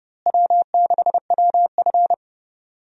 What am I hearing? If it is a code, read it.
Morse code, heard as a keyed tone, about 30 wpm, 700 Hz. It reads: W6WF
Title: USA Call Signs - 4 Characters - Repeated 2x - 30wpm